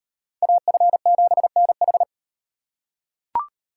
Morse code, heard as a keyed tone, about 36 wpm, 700 Hz. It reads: AF7NH